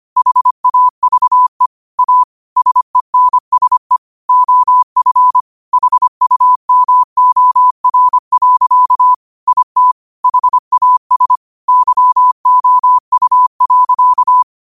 Morse code, heard as a keyed tone, about 25 wpm, 1000 Hz. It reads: HAVE A SENSE OF HUMOR. IT HAS YOU.
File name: qotdfiles / 25wpm